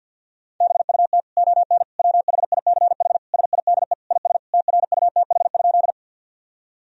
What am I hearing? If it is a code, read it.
Morse code, heard as a keyed tone, about 46 wpm, 700 Hz. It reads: BUT ON WHICH SIDE IS TRUTH?
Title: Sentences from Top 1000 Words - Review Part 1 - 50wpm